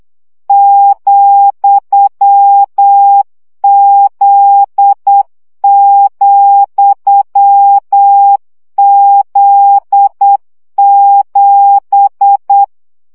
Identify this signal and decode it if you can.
Morse code, heard as a keyed tone, about 8 wpm, 800 Hz. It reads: ,Z,Z7
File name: CW gyakorlas z 7 vesszo 7 baud